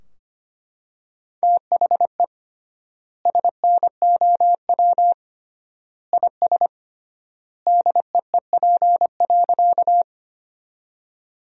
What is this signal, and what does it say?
Morse code, heard as a keyed tone, about 25 wpm, 700 Hz. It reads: THE SNOW IS DEEP.
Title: 3x Word-Spacing - Sentences from Top 500 Words - 25wpm